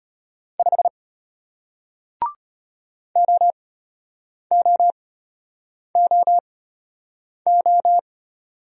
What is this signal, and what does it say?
Morse code, heard as a keyed tone, about 33 wpm, spaced as for 8 wpm, 700 Hz. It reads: 5 OOOO